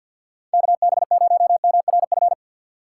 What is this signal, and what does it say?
Morse code, heard as a keyed tone, about 52 wpm, 700 Hz. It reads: KB0MRF
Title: USA Call Signs - 3x Character Spacing - 50wpm